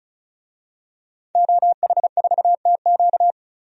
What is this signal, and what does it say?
Morse code, heard as a keyed tone, about 35 wpm, 700 Hz. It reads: OH4TQ
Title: DXCC-100 Call Signs - 5 Characters - 35wpm